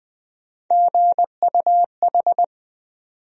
Morse code, heard as a keyed tone, about 20 wpm, 700 Hz. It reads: GUH
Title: USA Call Sign Postfixes - 3 Letters 20wpm